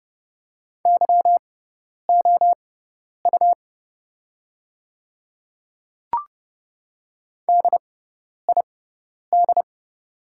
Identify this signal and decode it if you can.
Morse code, heard as a keyed tone, about 29 wpm, spaced as for 10 wpm, 690 Hz. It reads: YOU DID
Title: 6x Character Spacing - Top 200 Words - 30wpm